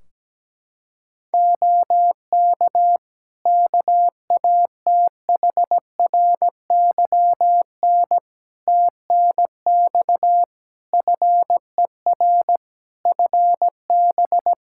Morse code, heard as a keyed tone, about 17 wpm, 700 Hz. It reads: OK KATHRYN TNX FER FB
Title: K1USN SST QSO Practice 17wpm